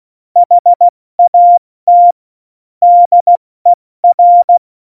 Morse code, heard as a keyed tone, about 16 wpm, 700 Hz. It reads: HAT DER
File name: Morsekurs / Witze / Witz 03. November 2020, 70 F80